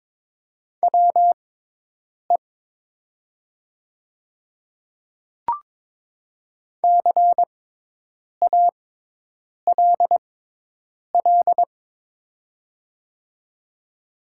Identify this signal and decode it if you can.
Morse code, heard as a keyed tone, about 22 wpm, spaced as for 8 wpm, 700 Hz. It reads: WE CALL
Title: 6x Character Spacing - Top 200 Words - 22wpm